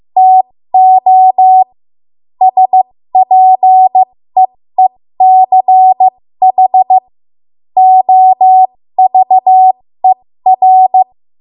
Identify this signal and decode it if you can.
Morse code, heard as a keyed tone, about 15 wpm, spaced as for 13 wpm, 750 Hz. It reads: TO SPEECH OVER
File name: Morse / 13WPM